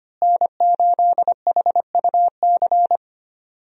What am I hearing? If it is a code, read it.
Morse code, heard as a keyed tone, about 25 wpm, 700 Hz. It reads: N8HUC
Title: USA Call Signs - 5 Characters - Repeated 3x - 25wpm